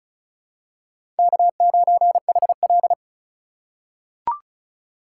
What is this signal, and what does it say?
Morse code, heard as a keyed tone, about 36 wpm, 700 Hz. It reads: K9HL
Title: USA Call Signs - 4 Characters - 4x Character Spacing - 35wpm